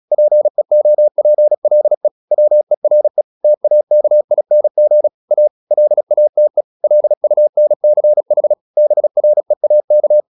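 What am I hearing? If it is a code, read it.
Morse code, heard as a keyed tone, about 36 wpm, 600 Hz. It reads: PEOPLE WERE TAKING A LATE LUNCH BREAK
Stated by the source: Morse practice at 36 wpm, text of Ken Follett's novel Never, from CW med Gnister.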